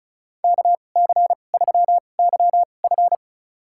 Morse code, heard as a keyed tone, about 36 wpm, spaced as for 25 wpm, 700 Hz. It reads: KC3YF
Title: USA Call Signs - 5 Characters - 2x Character Spacing - 35wpm